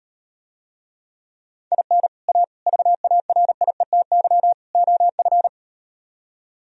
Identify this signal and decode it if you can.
Morse code, heard as a keyed tone, about 38 wpm, 700 Hz. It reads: IN A VARIETY OF